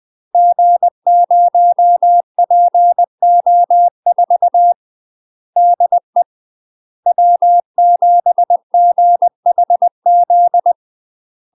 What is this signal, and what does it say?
Morse code, heard as a keyed tone, about 20 wpm, 700 Hz. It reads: G0PO4 DE W7GHZ
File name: UK MASTER QSOs TRAINING / 20WPM / UK MASTER QSOs 0067